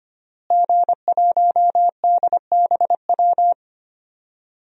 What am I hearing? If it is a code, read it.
Morse code, heard as a keyed tone, about 25 wpm, 700 Hz. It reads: G1DBW
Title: DXCC-100 Call Signs - 5 Characters - Repeated 3x - 25wpm